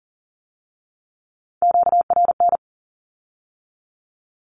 Morse code, heard as a keyed tone, about 38 wpm, 700 Hz. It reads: QRN
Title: Q-Codes with Spoken Meaning - 40wpm